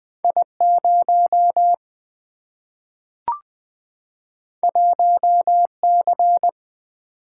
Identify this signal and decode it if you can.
Morse code, heard as a keyed tone, about 20 wpm, 700 Hz. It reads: I 0 1 C
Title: Single-Letter Single-Number 20wpm